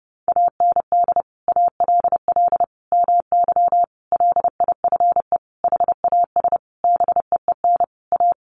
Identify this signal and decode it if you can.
Morse code, heard as a keyed tone, about 28 wpm, 700 Hz. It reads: AND ALL MY LIFE HAS BEEN A
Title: Sentences from Top 1500 Words 30wpm